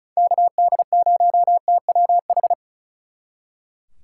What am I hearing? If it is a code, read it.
Morse code, heard as a keyed tone, about 46 wpm, 700 Hz. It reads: KD0TWH